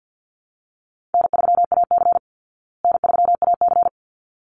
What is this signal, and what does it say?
Morse code, heard as a keyed tone, about 48 wpm, 700 Hz. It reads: N3AC N3AC
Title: Easy Call Signs Repeated 2x - 50wpm